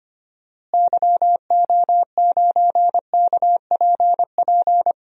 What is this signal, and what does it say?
Morse code, heard as a keyed tone, about 25 wpm, 700 Hz. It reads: YO9KPP
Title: DXCC-100 Call Signs - 6 Characters - Repeated 2x - 25wpm